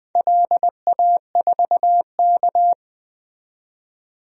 Morse code, heard as a keyed tone, about 20 wpm, 700 Hz. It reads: LA4K
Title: DXCC-100 Call Signs - 4 Characters - 20wpm